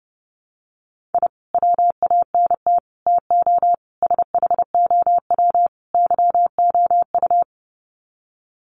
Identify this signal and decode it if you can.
Morse code, heard as a keyed tone, about 30 wpm, 700 Hz. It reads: I WANT TO SHOW YOU
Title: Sets of 5 Words 30wpm